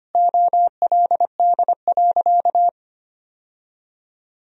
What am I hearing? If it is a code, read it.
Morse code, heard as a keyed tone, about 25 wpm, 700 Hz. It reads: OLD.
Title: Sentences from Top 1000 Words - Review Part 1 - 25wpm